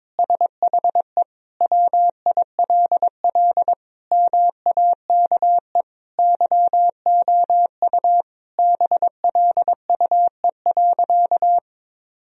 Morse code, heard as a keyed tone, about 21 wpm, 700 Hz. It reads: SHE WILL MAKE YOU BLUE.
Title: Sentences from Top 1000 Words - Review Part 3 - 22wpm